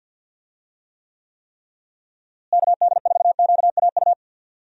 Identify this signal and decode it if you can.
Morse code, heard as a keyed tone, about 50 wpm, 700 Hz. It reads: KD4QAU
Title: USA Call Signs - 6 Characters - Repeated 3x - 50wpm